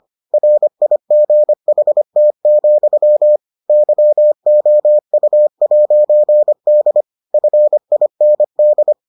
Morse code, heard as a keyed tone, about 25 wpm, 600 Hz. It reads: RIGHT, YOU'D FIND